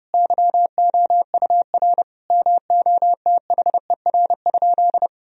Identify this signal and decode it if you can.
Morse code, heard as a keyed tone, about 30 wpm, 700 Hz. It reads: YOUR MOTHER?
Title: Sentences from Top 1000 Words - Review Part 2 - 30wpm